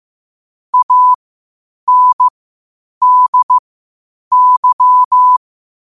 Morse code, heard as a keyed tone, about 15 wpm, spaced as for 8 wpm, 1000 Hz. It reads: ANDY
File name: qotdfiles / 8wpm